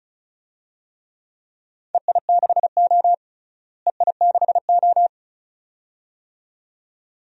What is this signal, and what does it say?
Morse code, heard as a keyed tone, about 36 wpm, 700 Hz. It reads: EI6O EI6O
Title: DXCC-100 Call Signs - 4 Characters - Repeated 2x - 35wpm